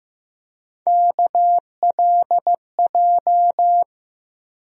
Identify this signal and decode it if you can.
Morse code, heard as a keyed tone, about 15 wpm, 700 Hz. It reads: KLJ
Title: USA Call Sign Postfixes - 3 Letters 15wpm